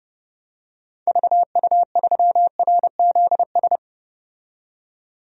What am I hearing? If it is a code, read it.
Morse code, heard as a keyed tone, about 30 wpm, 700 Hz. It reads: VU3RZS
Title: DXCC-100 Call Signs - 6 Characters - 30wpm